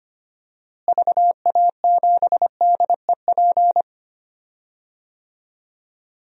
Morse code, heard as a keyed tone, about 25 wpm, 700 Hz. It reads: VA7DEP
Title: DXCC-100 Call Signs - 6 Characters - Repeated 3x - 25wpm